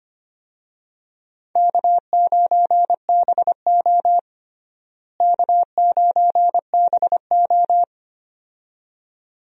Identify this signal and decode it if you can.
Morse code, heard as a keyed tone, about 25 wpm, 700 Hz. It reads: K9BO K9BO
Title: USA Call Signs - 4 Characters - Repeated 2x - 25wpm